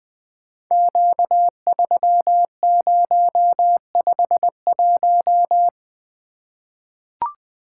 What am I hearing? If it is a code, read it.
Morse code, heard as a keyed tone, about 20 wpm, 700 Hz. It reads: Q3051
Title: Code Groups - 5 Letter:Numbers - 20wpm